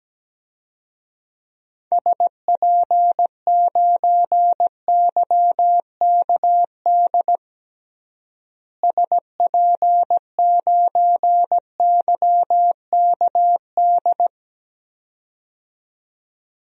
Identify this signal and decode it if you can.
Morse code, heard as a keyed tone, about 17 wpm, 700 Hz. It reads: SP9YKD SP9YKD
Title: DXCC-100 Call Signs - 6 Characters - Repeated 2x - 17wpm